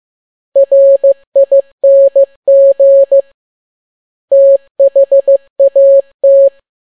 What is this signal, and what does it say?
Morse code, heard as a keyed tone, about 15 wpm, 550 Hz. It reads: RING THAT
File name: n6tt p1 15C13